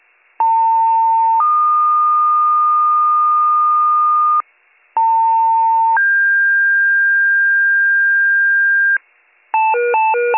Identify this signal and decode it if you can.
Two-tone page, heard as a one-second tone, then a three-second tone, about 900 then 1220 Hz, about 900 then 1600 Hz, with noise. It is heard from 0.4 to 9.0 s.